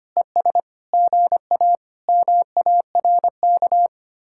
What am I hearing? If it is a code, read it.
Morse code, heard as a keyed tone, about 31 wpm, 700 Hz. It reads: ES GA MARK
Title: Speed-Racing Common QSO Phrases - 25wpm